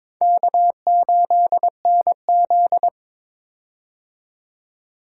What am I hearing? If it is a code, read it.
Morse code, heard as a keyed tone, about 23 wpm, 700 Hz. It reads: K8NZ
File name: Four Letter-Number Combinations - 22wpm